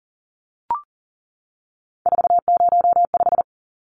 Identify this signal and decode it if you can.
Morse code, heard as a keyed tone, about 40 wpm, 700 Hz. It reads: 405
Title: Numbers - 3-Digit 40wpm